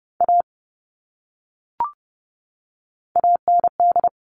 Morse code, heard as a keyed tone, about 30 wpm, 700 Hz. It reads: A AND
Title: Top 500 Words 30wpm